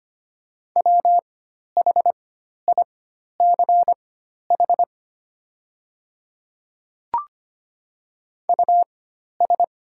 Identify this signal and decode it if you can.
Morse code, heard as a keyed tone, about 25 wpm, spaced as for 12 wpm, 690 Hz. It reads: WHICH US
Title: 4x Character Spacing - Top 200 Words - 25wpm